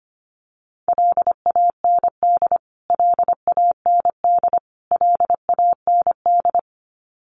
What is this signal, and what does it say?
Morse code, heard as a keyed tone, about 25 wpm, 700 Hz. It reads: LAND LAND LAND
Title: Top 200 Words Repeated 25wpm